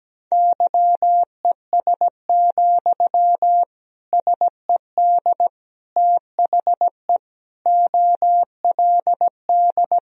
Morse code, heard as a keyed tone, about 17 wpm, 700 Hz. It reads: YES, SED THE OLD
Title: Sentences from Top 500 Words and Abbreviations 17wpm